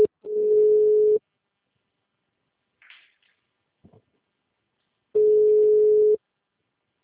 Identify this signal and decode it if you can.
Ringback tone (Europe, 425 Hz).